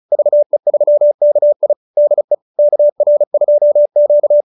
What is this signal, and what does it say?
Morse code, heard as a keyed tone, about 34 wpm, 590 Hz. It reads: VE3KI DE KR2Q